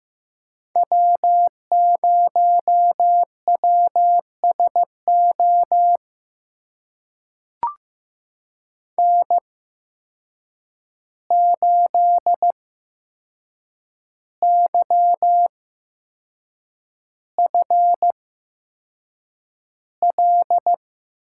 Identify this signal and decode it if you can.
Morse code, heard as a keyed tone, about 15 wpm, spaced as for 6 wpm, 700 Hz. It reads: W0WSO N8YFL
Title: USA Call Signs - 5 Characters - 8x Character Spacing - 15wpm